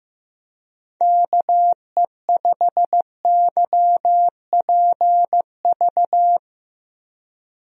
Morse code, heard as a keyed tone, about 15 wpm, 700 Hz. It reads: KE5YPV
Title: USA Call Signs - 6 Characters - 15wpm